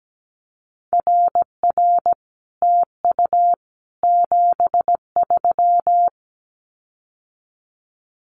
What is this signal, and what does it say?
Morse code, heard as a keyed tone, about 17 wpm, 700 Hz. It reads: RR TU 73
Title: Common QSO Phrases Encore 17wpm